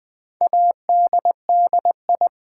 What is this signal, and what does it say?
Morse code, heard as a keyed tone, about 20 wpm, 700 Hz. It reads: ADDI
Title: Build A Word - 4 Letters At A Time - 12-Letter Words - 20wpm